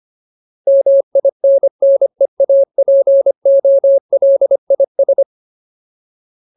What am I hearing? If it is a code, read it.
Morse code, heard as a keyed tone, about 25 wpm, 550 Hz. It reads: MINNEAPOLIS
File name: Ultimate Word List / 25WPM Ultimate